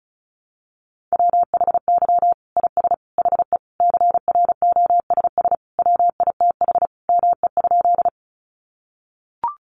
Morse code, heard as a keyed tone, about 34 wpm, 700 Hz. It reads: WHY IS HE CROSS WITH ME?
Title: Sentences from Top 300 Words 35wpm